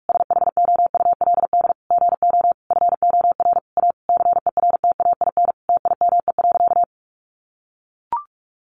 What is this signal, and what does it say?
Morse code, heard as a keyed tone, about 44 wpm, 700 Hz. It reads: SHOULD GO FOR A CERTAIN TIME.
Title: Sentences from Top 1000 Words 45wpm